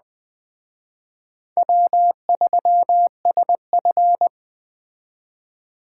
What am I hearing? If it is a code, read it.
Morse code, heard as a keyed tone, about 20 wpm, 700 Hz. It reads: W3SF